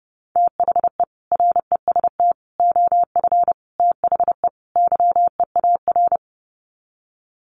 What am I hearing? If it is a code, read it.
Morse code, heard as a keyed tone, about 30 wpm, 700 Hz. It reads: THE REST OF THE YEAR
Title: Sets of 5 Words Encore 30wpm